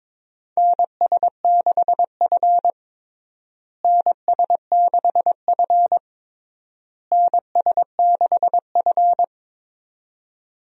Morse code, heard as a keyed tone, about 22 wpm, 700 Hz. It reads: NS6F NS6F NS6F
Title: USA Call Signs - 4 Characters - Repeated 3x - 22wpm